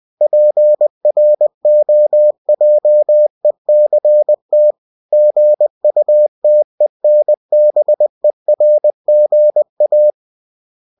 Morse code, heard as a keyed tone, about 20 wpm, 600 Hz. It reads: PROJECT GUTENBERGA
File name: cw-books / winnie-the-pooh / winnie-20wpm-page-0117